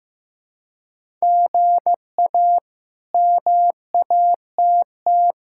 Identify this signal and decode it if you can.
Morse code, heard as a keyed tone, about 15 wpm, 700 Hz. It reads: GA MATT
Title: K1USN SST QSO Practice - 15wpm